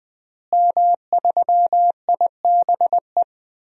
Morse code, heard as a keyed tone, about 20 wpm, 700 Hz. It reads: M3IBE
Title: DXCC-100 Call Signs - 5 Characters - Repeated 3x - 20wpm